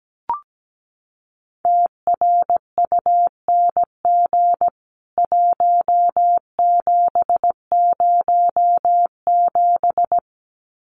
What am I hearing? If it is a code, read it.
Morse code, heard as a keyed tone, about 17 wpm, 700 Hz. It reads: TRUNG 1707
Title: CWT Exchange 17wpm